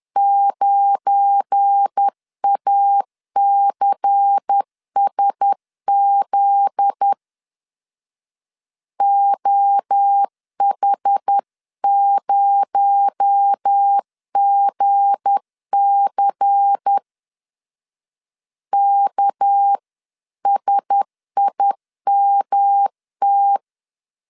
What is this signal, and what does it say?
Morse code, heard as a keyed tone, about 10 wpm, 800 Hz. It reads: 9ACSZ OH0GC KSIMT
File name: LEZIONI / 7 GF9Z / 7 GF9Z 10wpm 2611